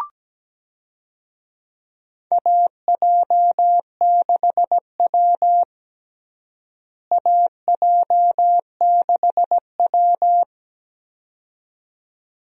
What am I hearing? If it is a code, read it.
Morse code, heard as a keyed tone, about 17 wpm, 700 Hz. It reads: AJ6W AJ6W
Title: USA Call Signs - 4 Characters - Repeated 2x - 17wpm